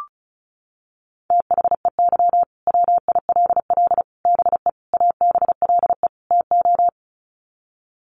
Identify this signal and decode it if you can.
Morse code, heard as a keyed tone, about 34 wpm, 700 Hz. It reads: THEY WILL BE ABLE TO
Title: Sets of 5 Words Encore 35wpm